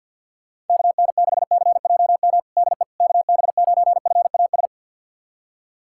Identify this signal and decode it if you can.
Morse code, heard as a keyed tone, about 48 wpm, 700 Hz. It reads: KN6CJM DE KB9FAS
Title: USA Callsign DE Callsign 50wpm